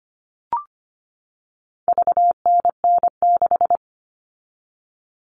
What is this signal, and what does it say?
Morse code, heard as a keyed tone, about 25 wpm, 700 Hz. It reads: VNN6
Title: All Cut Numbers - 4-Digit 25wpm